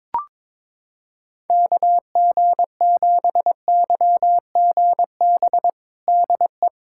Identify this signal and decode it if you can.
Morse code, heard as a keyed tone, about 22 wpm, 700 Hz. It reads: KG7YGB DE
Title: USA Callsign DE Callsign 22wpm